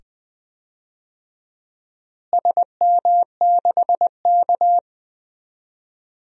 Morse code, heard as a keyed tone, about 19 wpm, 700 Hz. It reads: SM6K
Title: DXCC-100 Call Signs - 4 Characters - Repeated 2x - 20wpm